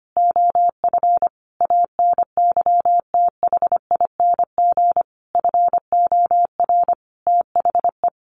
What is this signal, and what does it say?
Morse code, heard as a keyed tone, about 25 wpm, 700 Hz. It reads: OF ANYTHING FOR THE
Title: Sentences from Top 1500 Words 25wpm